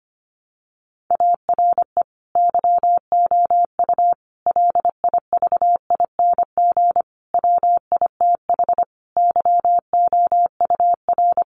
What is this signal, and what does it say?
Morse code, heard as a keyed tone, about 25 wpm, 700 Hz. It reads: ARE YOU LIVING WITH YOUR
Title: Sentences from Top 1100 Words 25wpm